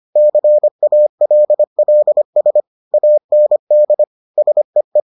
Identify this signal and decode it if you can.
Morse code, heard as a keyed tone, about 25 wpm, 600 Hz. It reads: CALLS AND SEE